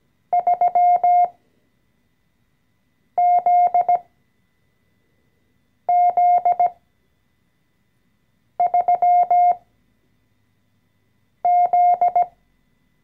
Morse code, heard as a keyed tone, about 17 wpm, spaced as for 4 wpm, 710 Hz. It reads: 3ZZ3Z
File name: CW-Online-Kurs / Anfänger / Lektion 014